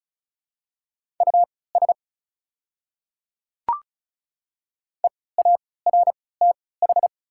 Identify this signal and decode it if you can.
Morse code, heard as a keyed tone, about 34 wpm, spaced as for 20 wpm, 700 Hz. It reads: US EARTH
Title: 3x Character Spacing - Top 200 Words - 35wpm